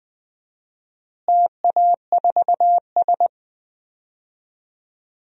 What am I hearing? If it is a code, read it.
Morse code, heard as a keyed tone, about 20 wpm, 700 Hz. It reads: TA4S
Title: DXCC-100 Call Signs - 4 Characters - Repeated 3x - 20wpm